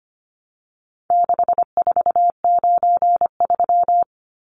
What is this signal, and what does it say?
Morse code, heard as a keyed tone, about 25 wpm, 700 Hz. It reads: 6493
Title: Numbers - 4-Digit 25wpm